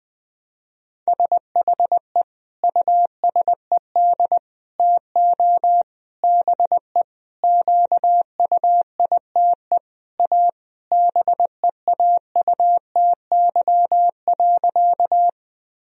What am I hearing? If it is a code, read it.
Morse code, heard as a keyed tone, about 21 wpm, 700 Hz. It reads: SHE USED TO BE QUITE A BEAUTY.
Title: Sentences from Top 1000 Words - Review Part 2 - 20wpm